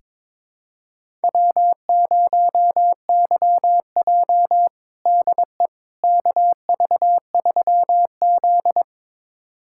Morse code, heard as a keyed tone, about 22 wpm, 700 Hz. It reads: W0YJ DE KV3Z
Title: USA Callsign DE Callsign - 4 Characters - 22wpm